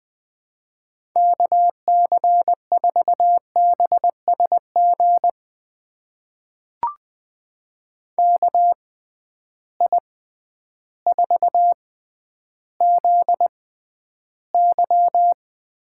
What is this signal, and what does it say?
Morse code, heard as a keyed tone, about 21 wpm, spaced as for 11 wpm, 700 Hz. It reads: KC4BSG KI4ZY
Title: USA Call Signs - 6 Characters - 6x Character Spacing - 20wpm